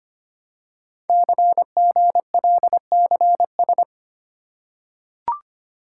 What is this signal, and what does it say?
Morse code, heard as a keyed tone, about 25 wpm, 700 Hz. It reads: CGLCS
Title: Code Groups - 5 Letters - 25wpm